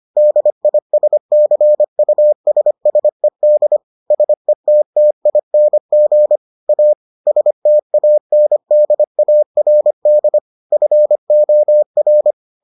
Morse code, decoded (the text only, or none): DISCUSSED SETTING A STANDARD FOR